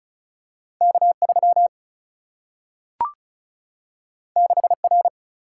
Morse code, heard as a keyed tone, about 36 wpm, 700 Hz. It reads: K3 6R
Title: Single-Letter Single-Number 35wpm